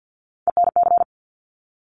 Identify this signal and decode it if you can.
Morse code, heard as a keyed tone, about 50 wpm, 700 Hz. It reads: ENC
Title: Top Three Letter Combinations in Words 50wpm